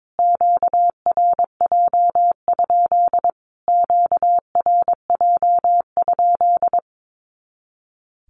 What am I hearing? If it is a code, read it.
Morse code, heard as a keyed tone, about 22 wpm, 700 Hz. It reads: QRJ? QRJ?
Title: Q-Codes Repeated 3x 22wpm